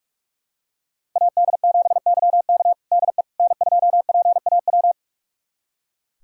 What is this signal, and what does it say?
Morse code, heard as a keyed tone, about 46 wpm, 700 Hz. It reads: AD7YK DE N2PAW
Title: USA Callsign DE Callsign - 5 Characters - 45wpm